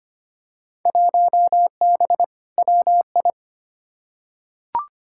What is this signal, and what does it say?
Morse code, heard as a keyed tone, about 25 wpm, 700 Hz. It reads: 1B WI
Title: Field Day Exchanges - 25wpm